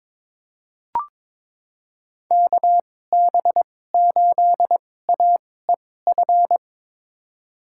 Morse code, heard as a keyed tone, about 22 wpm, spaced as for 16 wpm, 700 Hz. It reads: KB8AEF
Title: USA Call Signs - 6 Characters - 2x Character Spacing - 22wpm